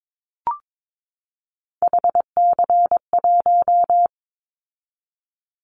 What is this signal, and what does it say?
Morse code, heard as a keyed tone, about 22 wpm, 700 Hz. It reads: HC1
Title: Callsign Prefixes for Top 100 DXCC - 22wpm